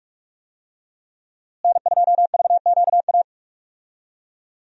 Morse code, heard as a keyed tone, about 42 wpm, 700 Hz. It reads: N2VQA
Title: USA Call Signs - 5 Characters - Repeated 2x - 45wpm